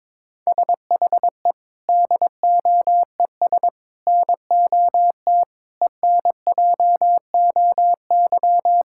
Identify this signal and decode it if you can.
Morse code, heard as a keyed tone, about 22 wpm, 700 Hz. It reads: SHE DOES NOT ENJOY
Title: Sentences from Top 1000 Words - Review Part 5 - 22wpm